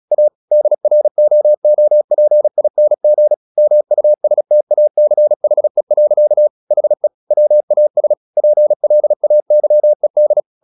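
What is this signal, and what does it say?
Morse code, heard as a keyed tone, about 36 wpm, 600 Hz. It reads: A DROOPING MUSTACHE. HE WAS PLAYED
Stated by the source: Morse practice at 36 wpm, text of Ken Follett's novel Never, from CW med Gnister.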